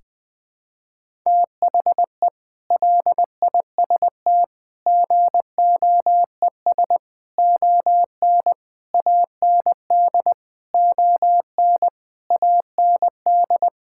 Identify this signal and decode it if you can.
Morse code, heard as a keyed tone, about 20 wpm, 700 Hz. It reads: THE LIST GOES ON AND ON AND
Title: Sets of 8 Words - 20wpm